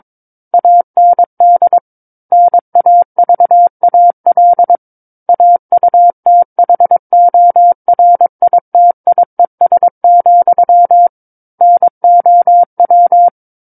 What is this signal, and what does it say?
Morse code, heard as a keyed tone, about 22 wpm, 700 Hz. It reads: AND NAVAL AUTHORITIES, NOW
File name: Book One / 13-Chapter 13 - 22 WPM